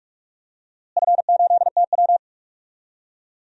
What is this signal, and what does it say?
Morse code, heard as a keyed tone, about 41 wpm, 700 Hz. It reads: F8TW
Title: DXCC-100 Call Signs - 4 Characters - Repeated 2x - 45wpm